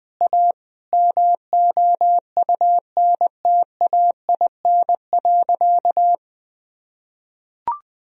Morse code, heard as a keyed tone, about 20 wpm, 700 Hz. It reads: A MOUNTAIN.
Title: Sentences from Top 1000 Words - Review Part 1 - 20wpm